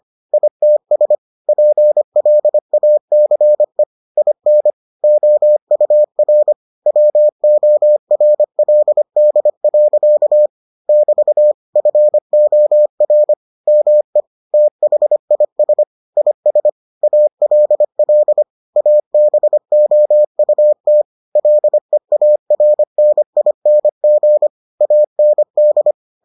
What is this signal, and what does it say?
Morse code, heard as a keyed tone, about 25 wpm, 600 Hz. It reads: ITS PLACE IN OUR WORLD. <BT> FOR ME THIS IS ALL ABOUT LEARNING AND